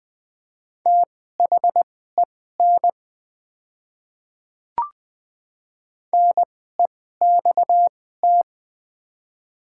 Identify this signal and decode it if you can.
Morse code, heard as a keyed tone, about 20 wpm, spaced as for 14 wpm, 700 Hz. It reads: THEN NEXT
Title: 2x Character Spacing - Top 300 Words - 20wpm